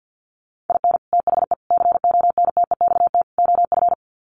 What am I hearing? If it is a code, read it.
Morse code, heard as a keyed tone, about 50 wpm, 700 Hz. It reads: IN THE CONTEXT OF
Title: Sets of 4 Words 50wpm